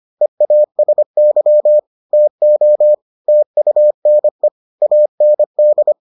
Morse code, heard as a keyed tone, about 25 wpm, 600 Hz. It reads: EASY TO TUNE AND